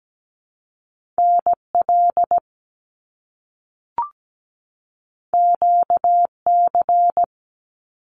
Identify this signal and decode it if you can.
Morse code, heard as a keyed tone, about 17 wpm, 700 Hz. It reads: NL QC
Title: Canadian Provinces and Territories 17wpm